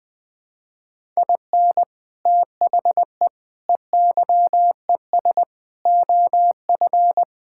Morse code, heard as a keyed tone, about 20 wpm, 700 Hz. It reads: IN THE EYES OF
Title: Sentences with 5-Word Phrases - 20wpm